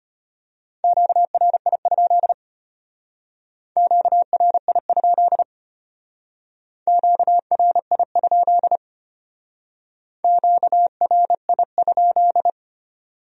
Morse code, heard as a keyed tone, about 32 wpm, 700 Hz. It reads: QRI? QRI? QRI? QRI?